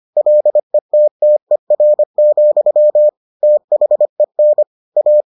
Morse code, heard as a keyed tone, about 25 wpm, 600 Hz. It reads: LETTER, THEN A